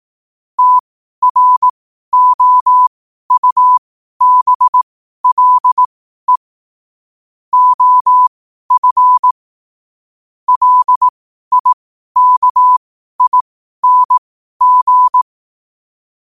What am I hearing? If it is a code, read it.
Morse code, heard as a keyed tone, about 18 wpm, spaced as for 12 wpm, 1000 Hz. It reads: TROUBLE OF LIKING